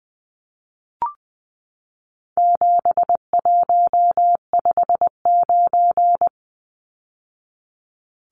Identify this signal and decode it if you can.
Morse code, heard as a keyed tone, about 19 wpm, 700 Hz. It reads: 7159
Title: Numbers - 4-Digit 20wpm